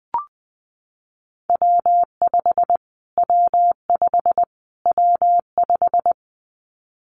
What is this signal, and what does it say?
Morse code, heard as a keyed tone, about 20 wpm, 700 Hz. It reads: W5 W5 W5
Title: Callsign Prefixes for Top 25 DXCC - Repeated 3x - 20wpm